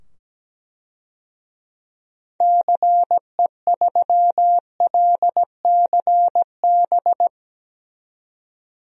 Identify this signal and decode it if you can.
Morse code, heard as a keyed tone, about 17 wpm, 700 Hz. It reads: CE3LCB
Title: DXCC-100 Call Signs - 6 Characters - 17wpm